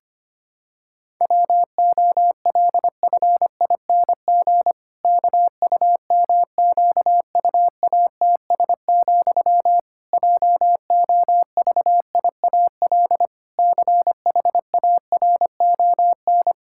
Morse code, heard as a keyed tone, about 25 wpm, 700 Hz. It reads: WOLFING KUMQUATS, JOVIAL CHARON
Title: Pangram Challenge v2 - 25wpm